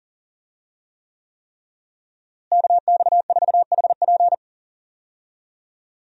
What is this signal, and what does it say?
Morse code, heard as a keyed tone, about 40 wpm, 700 Hz. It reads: KX4HP
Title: USA Call Signs - 5 Characters - Repeated 3x - 40wpm